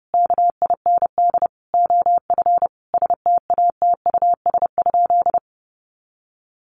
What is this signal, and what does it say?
Morse code, heard as a keyed tone, about 30 wpm, 700 Hz. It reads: KIND OF STATUS?
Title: Sentences from Top 1400 Words 30wpm